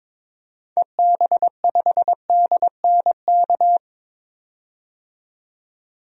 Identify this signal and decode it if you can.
Morse code, heard as a keyed tone, about 22 wpm, 700 Hz. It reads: EB5DNK
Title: DXCC-100 Call Signs - 6 Characters - 22wpm